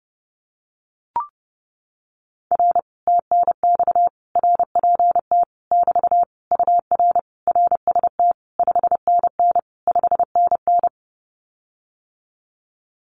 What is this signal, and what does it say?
Morse code, heard as a keyed tone, about 28 wpm, 700 Hz. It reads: R TNX RPT <BT> UR RST 5NN 5NN
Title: Common QSO Phrases Encore 30wpm